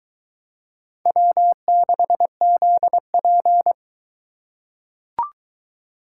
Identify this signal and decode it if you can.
Morse code, heard as a keyed tone, about 22 wpm, 700 Hz. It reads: W6ZP